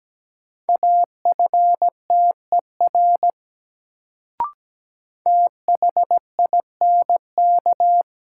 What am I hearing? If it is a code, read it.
Morse code, heard as a keyed tone, about 17 wpm, 700 Hz. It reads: AFTER THINK
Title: Sentences from Top 200 Words - New Word Drill - 17wpm